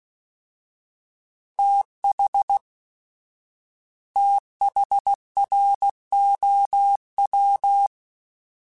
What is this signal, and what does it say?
Morse code, heard as a keyed tone, about 16 wpm, 800 Hz. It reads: TH THROW
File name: TH / TH mp3s / TH Prefixes 1x1 @15